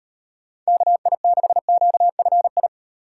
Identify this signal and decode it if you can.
Morse code, heard as a keyed tone, about 38 wpm, 700 Hz. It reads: KI6QFI